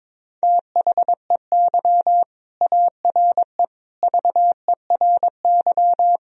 Morse code, heard as a keyed tone, about 21 wpm, 700 Hz. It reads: THEY ARE VERY
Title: Sentences from Top 1000 Words - Review Part 4 - 22wpm